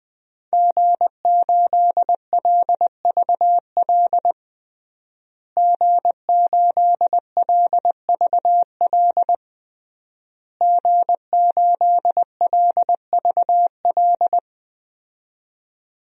Morse code, heard as a keyed tone, about 20 wpm, 700 Hz. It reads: G8LVL G8LVL G8LVL
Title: DXCC-100 Call Signs - 5 Characters - Repeated 3x - 20wpm